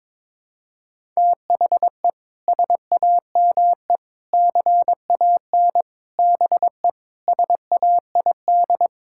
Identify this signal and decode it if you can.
Morse code, heard as a keyed tone, about 22 wpm, 700 Hz. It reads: THE SAME CAN BE SAID
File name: practice / 22wpm / Sets of 6 Words - 22wpm